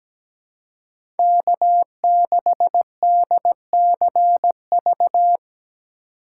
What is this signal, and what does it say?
Morse code, heard as a keyed tone, about 17 wpm, 700 Hz. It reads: K6DCV
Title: USA Call Signs - 5 Characters - 2x Character Spacing - 17wpm